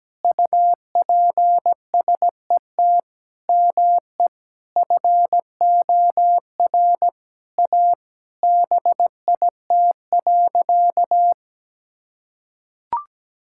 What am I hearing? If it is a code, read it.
Morse code, heard as a keyed tone, about 17 wpm, 700 Hz. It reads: UPSET ME FOR A BIT.
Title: Sentences from Top 1000 Words - Review Part 5 - 17wpm